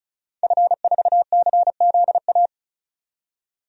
Morse code, heard as a keyed tone, about 35 wpm, 700 Hz. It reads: F4CZA
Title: DXCC-100 Call Signs - 5 Characters - 35wpm